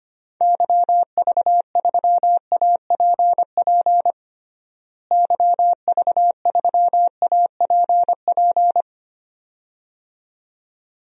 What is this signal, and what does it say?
Morse code, heard as a keyed tone, about 25 wpm, 700 Hz. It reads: YV3APP YV3APP
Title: DXCC-100 Call Signs - 6 Characters - Repeated 2x - 25wpm